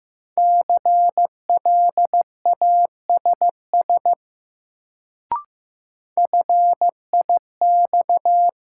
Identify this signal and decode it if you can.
Morse code, heard as a keyed tone, about 15 wpm, 700 Hz. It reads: CLASS FIX
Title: Sentences from Top 1400 Words - New Word Drill Alt - 15wpm